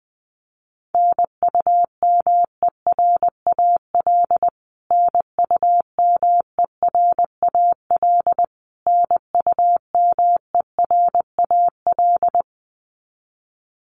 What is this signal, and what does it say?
Morse code, heard as a keyed tone, about 20 wpm, 700 Hz. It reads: NUMERAL NUMERAL NUMERAL
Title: Top 500 Words Repeated 20wpm